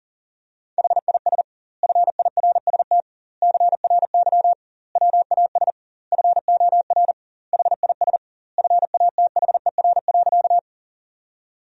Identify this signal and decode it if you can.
Morse code, heard as a keyed tone, about 40 wpm, 700 Hz. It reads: HIS FIRST CRY WAS FOR HIS FATHER.
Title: 2x Word-Spacing - Sentences from Top 500 Words - 40wpm